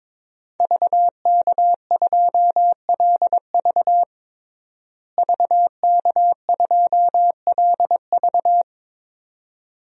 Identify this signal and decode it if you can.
Morse code, heard as a keyed tone, about 22 wpm, 700 Hz. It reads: VK2LV VK2LV
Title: DXCC-100 Call Signs - 5 Characters - Repeated 2x - 22wpm